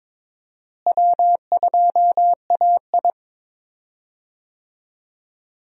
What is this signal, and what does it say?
Morse code, heard as a keyed tone, about 21 wpm, 700 Hz. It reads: W2AI
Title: USA Call Signs - 4 Characters - Repeated 3x - 22wpm